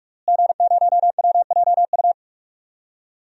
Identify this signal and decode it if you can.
Morse code, heard as a keyed tone, about 41 wpm, 700 Hz. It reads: G0WJU
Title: DXCC-100 Call Signs - 5 Characters - Repeated 3x - 45wpm